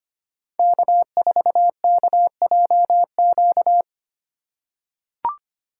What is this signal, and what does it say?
Morse code, heard as a keyed tone, about 25 wpm, 700 Hz. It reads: K4KJQ
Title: USA Call Signs - 5 Characters - 2x Character Spacing - 25wpm